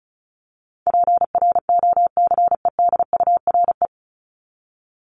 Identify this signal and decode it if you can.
Morse code, heard as a keyed tone, about 34 wpm, 700 Hz. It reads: PROCEDURE
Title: Top 1751-2000 Words 35wpm